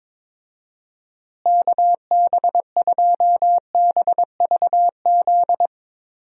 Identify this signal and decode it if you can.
Morse code, heard as a keyed tone, about 21 wpm, 700 Hz. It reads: KB2BVZ
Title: USA Call Signs - 6 Characters - Repeated 2x - 22wpm